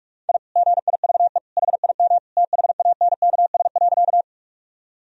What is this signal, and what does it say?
Morse code, heard as a keyed tone, about 42 wpm, 700 Hz. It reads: I GIVE HIM THANKS.
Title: Sentences from Top 1000 Words - Review Part 3 - 45wpm